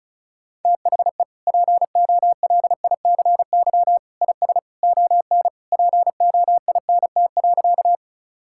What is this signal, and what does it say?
Morse code, heard as a keyed tone, about 34 wpm, 700 Hz. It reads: THE POLICY IS ON POINT.
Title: Sentences from Top 1000 Words - Review Part 5 - 35wpm